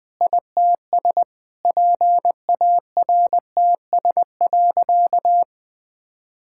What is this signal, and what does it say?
Morse code, heard as a keyed tone, about 21 wpm, 700 Hz. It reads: ITS PARTS.
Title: Sentences with 8-Word Phrases - 20wpm